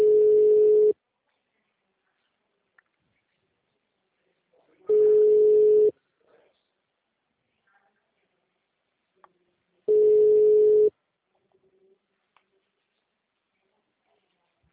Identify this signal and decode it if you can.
Ringback tone (Europe, 425 Hz).